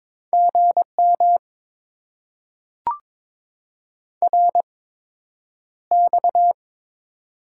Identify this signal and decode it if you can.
Morse code, heard as a keyed tone, about 22 wpm, spaced as for 9 wpm, 700 Hz. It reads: GM RX
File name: Top 90 QSO Elements - 8x Character Spacing - 22wpm